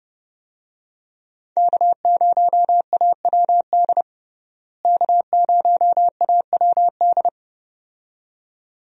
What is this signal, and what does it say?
Morse code, heard as a keyed tone, about 32 wpm, 700 Hz. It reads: K0AWD K0AWD
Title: USA Call Signs - 5 Characters - Repeated 2x - 30wpm